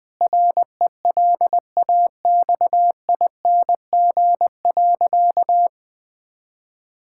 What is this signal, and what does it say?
Morse code, heard as a keyed tone, about 20 wpm, 700 Hz. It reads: RELAXING.
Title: Difficult Sentences with Hints - 20wpm